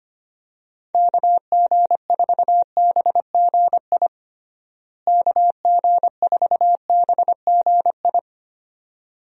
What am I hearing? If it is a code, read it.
Morse code, heard as a keyed tone, about 25 wpm, 700 Hz. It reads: KG4BGI KG4BGI
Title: USA Call Signs - 6 Characters - Repeated 2x - 25wpm